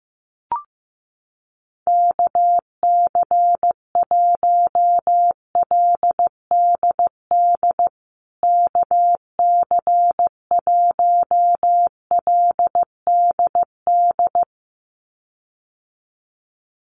Morse code, heard as a keyed tone, about 15 wpm, 700 Hz. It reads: KC1LDD KC1LDD
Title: USA Call Signs Repeated 2x 15wpm